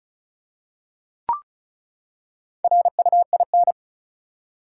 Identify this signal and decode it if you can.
Morse code, heard as a keyed tone, about 36 wpm, 700 Hz. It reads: RUIN
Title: Four Letter Words - 35wpm